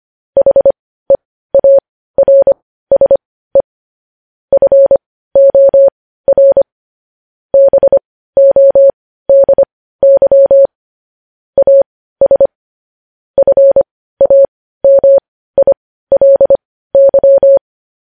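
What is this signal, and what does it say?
Morse code, heard as a keyed tone, about 25 wpm, spaced as for 15 wpm, 550 Hz. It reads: HEARSE FOR BODY AS FAMILY
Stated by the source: Morse code news headlines at 15 wpm.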